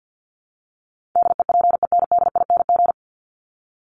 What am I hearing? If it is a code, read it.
Morse code, heard as a keyed tone, about 50 wpm, 700 Hz. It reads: DEPENDING
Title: ING Suffix - Top 100 Words with Base Words 50wpm